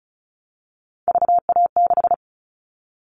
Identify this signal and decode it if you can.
Morse code, heard as a keyed tone, about 36 wpm, 700 Hz. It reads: VA6
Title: Callsign Prefixes for Top 25 DXCC - 35wpm